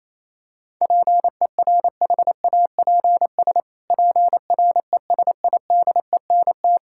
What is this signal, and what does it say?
Morse code, heard as a keyed tone, about 28 wpm, 700 Hz. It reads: PERHAPS PRESIDENT
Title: Pangram Challenge v2 - 28wpm